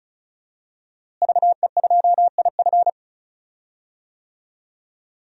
Morse code, heard as a keyed tone, about 36 wpm, 700 Hz. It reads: VE2IF
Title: DXCC-100 Call Signs - 5 Characters - Repeated 3x - 35wpm